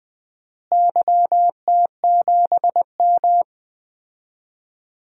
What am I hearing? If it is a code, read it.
Morse code, heard as a keyed tone, about 20 wpm, 700 Hz. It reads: YT7M
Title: DXCC-100 Call Signs - 4 Characters - Repeated 3x - 20wpm